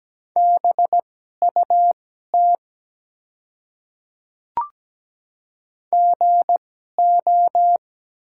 Morse code, heard as a keyed tone, about 17 wpm, spaced as for 12 wpm, 700 Hz. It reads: BUT GO
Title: 2x Character Spacing - Top 200 Words - 17wpm